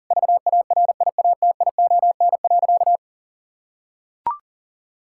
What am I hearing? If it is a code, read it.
Morse code, heard as a keyed tone, about 40 wpm, 700 Hz. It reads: VARIATION.
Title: Sentences from Top 1600 Words 40wpm